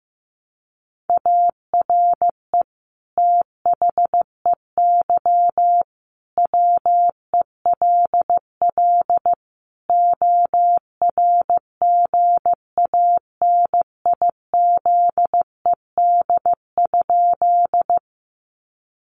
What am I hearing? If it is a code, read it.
Morse code, heard as a keyed tone, about 15 wpm, 700 Hz. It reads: ARE THEY WELL ORGANIZED?
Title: Sentences from Top 1300 Words 15wpm